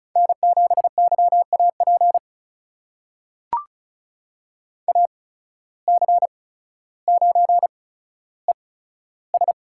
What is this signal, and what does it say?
Morse code, heard as a keyed tone, about 36 wpm, spaced as for 15 wpm, 700 Hz. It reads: N7YAP AC9ES
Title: USA Call Signs - 5 Characters - 8x Character Spacing - 35wpm